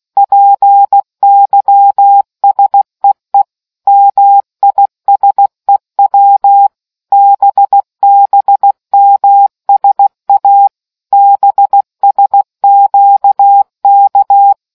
Morse code, heard as a keyed tone, about 16 wpm, 800 Hz. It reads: PYSEE MISEW BBMSA BSQK